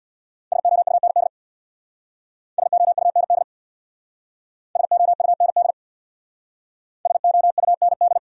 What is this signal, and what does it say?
Morse code, heard as a keyed tone, about 63 wpm, 700 Hz. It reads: SOUND SOUND SOUND SOUND